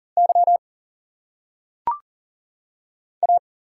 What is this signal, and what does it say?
Morse code, heard as a keyed tone, about 40 wpm, 700 Hz. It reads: Y A
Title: Single Letters - Phonetic Pronunciation 40wpm